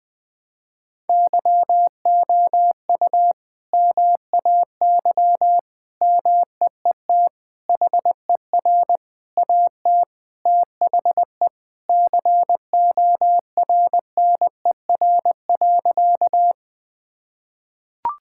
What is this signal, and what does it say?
Morse code, heard as a keyed tone, about 20 wpm, 700 Hz. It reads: YOU MAY MEET HER AT THE CORNER.
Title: Sentences from Top 1600 Words 20wpm